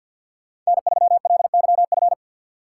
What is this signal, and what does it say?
Morse code, heard as a keyed tone, about 64 wpm, 700 Hz. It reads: N3LYF